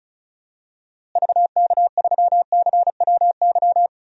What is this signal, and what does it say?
Morse code, heard as a keyed tone, about 34 wpm, 700 Hz. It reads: VK3CWY
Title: DXCC-100 Call Signs - 6 Characters - Repeated 2x - 35wpm